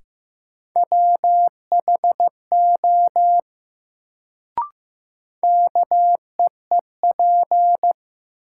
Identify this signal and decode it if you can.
Morse code, heard as a keyed tone, about 15 wpm, 700 Hz. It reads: WHO KEEP
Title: Sets of 2 Words - New Word Drill - 15wpm